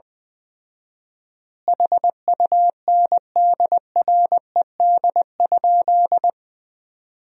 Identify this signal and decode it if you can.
Morse code, heard as a keyed tone, about 20 wpm, 700 Hz. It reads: HUNDRED?
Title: 4x Word-Spacing - Sentences from Top 400 Words - 20wpm